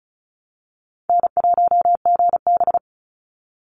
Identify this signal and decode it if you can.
Morse code, heard as a keyed tone, about 34 wpm, 700 Hz. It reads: N1GB
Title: NA SOTA Chaser Callsigns 35wpm